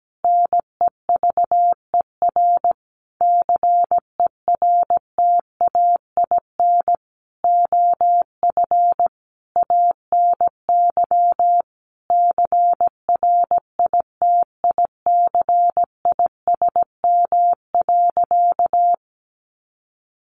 Morse code, heard as a keyed tone, about 17 wpm, 700 Hz. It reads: NEVER CERTAIN OF ANY CRITICISM.
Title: Sentences from Top 1300 Words 17wpm